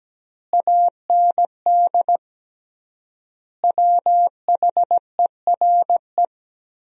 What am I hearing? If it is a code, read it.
Morse code, heard as a keyed tone, about 17 wpm, 700 Hz. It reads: AND WHERE
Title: 3x Word-Spacing - Sentences from Top 300 Words - 17wpm